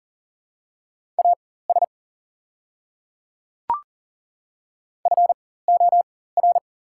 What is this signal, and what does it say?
Morse code, heard as a keyed tone, about 40 wpm, spaced as for 19 wpm, 700 Hz. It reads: AS FOR
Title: 4x Character Spacing - Top 100 Words - 40wpm